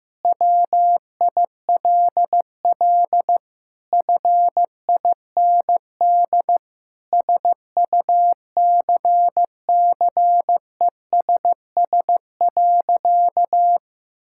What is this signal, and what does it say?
Morse code, heard as a keyed tone, about 15 wpm, 700 Hz. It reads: WILL FIND SUCCESS.
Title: Sentences from Top 1000 Words - Review Part 6 - 15wpm